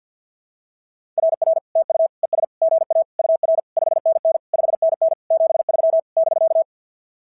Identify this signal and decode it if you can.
Morse code, heard as a keyed tone, about 46 wpm, 660 Hz. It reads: RR TU ES GA UR 5NN 5NN 73 <BK>
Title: SOTA QSO Practice 50wpm